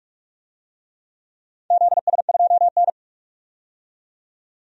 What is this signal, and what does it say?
Morse code, heard as a keyed tone, about 44 wpm, 690 Hz. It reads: ZS2N
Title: DXCC-100 Call Signs - 4 Characters - 45wpm